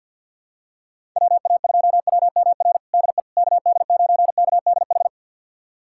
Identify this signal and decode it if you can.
Morse code, heard as a keyed tone, about 49 wpm, 690 Hz. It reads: WA2WMR DE KD9KDH
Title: USA Callsign DE Callsign - 6 Characters - 50wpm